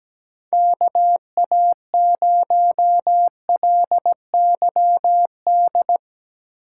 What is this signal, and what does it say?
Morse code, heard as a keyed tone, about 17 wpm, 700 Hz. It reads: KA0LYD
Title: USA Call Signs - 4x Character Spacing - 17wpm